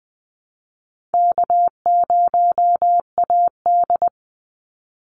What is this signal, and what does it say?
Morse code, heard as a keyed tone, about 20 wpm, 700 Hz. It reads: K0AD
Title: Easy Call Signs Repeated 2x - 20wpm